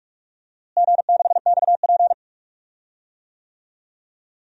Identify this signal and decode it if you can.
Morse code, heard as a keyed tone, about 44 wpm, 690 Hz. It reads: G6XP
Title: DXCC-100 Call Signs - 4 Characters - Repeated 3x - 45wpm